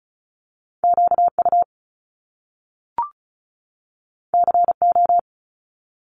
Morse code, heard as a keyed tone, about 36 wpm, 700 Hz. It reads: QU CO
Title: Top Two Letter Combinations in Words 35wpm